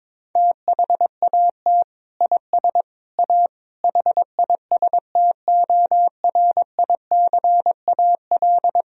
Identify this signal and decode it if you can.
Morse code, heard as a keyed tone, about 22 wpm, 700 Hz. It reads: THAT IS A HISTORICAL
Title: Sentences from Top 1000 Words - Review Part 4 - 22wpm